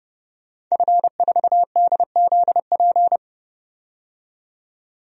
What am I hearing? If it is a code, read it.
Morse code, heard as a keyed tone, about 30 wpm, 700 Hz. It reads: F4DZP
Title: DXCC-100 Call Signs - 5 Characters - 30wpm